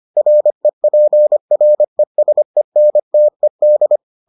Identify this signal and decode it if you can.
Morse code, heard as a keyed tone, about 25 wpm, 600 Hz. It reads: REPRESENTED